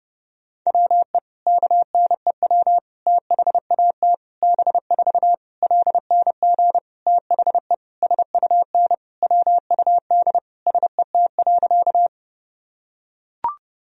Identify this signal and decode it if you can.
Morse code, heard as a keyed tone, about 30 wpm, 700 Hz. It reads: WE KNEW THAT B4 LNG THE SUN WUD SET.
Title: Sentences from Top 400 Words and Abbreviations 30wpm